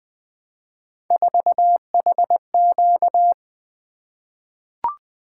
Morse code, heard as a keyed tone, about 20 wpm, 700 Hz. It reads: 4HQ
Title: Three Letter-Number Combinations - 20wpm